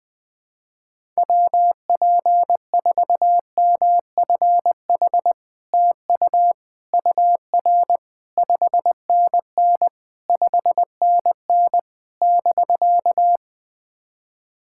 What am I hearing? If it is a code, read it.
Morse code, heard as a keyed tone, about 20 wpm, 700 Hz. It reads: WP4MFH TU UR 5NN 5NN <BK>
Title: SOTA QSO Practice 20wpm